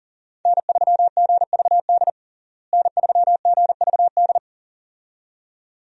Morse code, heard as a keyed tone, about 40 wpm, 700 Hz. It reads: N3GVD N3GVD
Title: USA Call Signs - 5 Characters - Repeated 2x - 40wpm